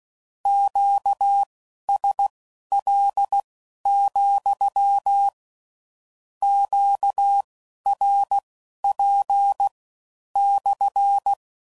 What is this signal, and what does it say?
Morse code, heard as a keyed tone, about 16 wpm, spaced as for 11 wpm, 800 Hz. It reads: QSL, QRP /
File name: CW Code Groups @15WPM / Q Calls / Q Calls SPSL